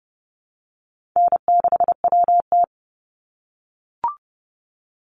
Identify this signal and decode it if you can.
Morse code, heard as a keyed tone, about 30 wpm, 700 Hz. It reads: N6WT
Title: NA SOTA Chaser Callsigns 30wpm